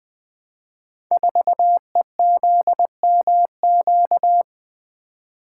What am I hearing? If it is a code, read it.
Morse code, heard as a keyed tone, about 20 wpm, 700 Hz. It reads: 4EZMQ
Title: Code Groups - 5 Letter:Numbers - 20wpm